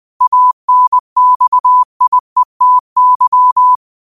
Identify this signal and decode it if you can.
Morse code, heard as a keyed tone, about 20 wpm, 1000 Hz. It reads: ANXIETY